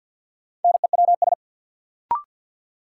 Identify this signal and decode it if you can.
Morse code, heard as a keyed tone, about 53 wpm, 700 Hz. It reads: NEWS
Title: Sets of 5 Words - New Word Drill - 50wpm